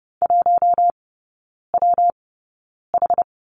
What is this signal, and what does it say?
Morse code, heard as a keyed tone, about 30 wpm, spaced as for 9 wpm, 700 Hz. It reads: 1WH
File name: Lesson 19e - Letter w - Three Characters 30wpm